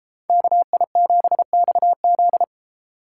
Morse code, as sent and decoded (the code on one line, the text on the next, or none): -.- .. --... -..- --..
KI7XZ